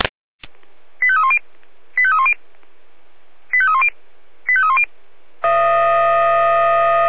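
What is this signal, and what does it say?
Five-tone selective call (ZVEI), heard as quick runs of five tones, tones of about 70 ms. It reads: 85310 85310 85310 85310
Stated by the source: ZVEI five-tone selective call received on an SDR.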